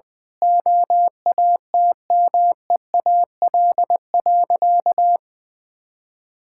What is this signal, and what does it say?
Morse code, heard as a keyed tone, about 21 wpm, 700 Hz. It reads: OATMEAL.
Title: Easy Sentences with Hints - 20wpm